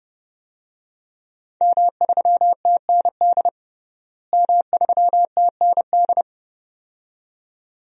Morse code, heard as a keyed tone, about 30 wpm, 700 Hz. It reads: M3TND M3TND
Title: DXCC-100 Call Signs - 5 Characters - Repeated 2x - 30wpm